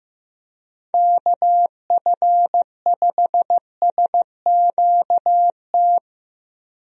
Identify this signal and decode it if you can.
Morse code, heard as a keyed tone, about 15 wpm, 700 Hz. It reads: KF5SQT
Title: USA Call Signs - 6 Characters - 15wpm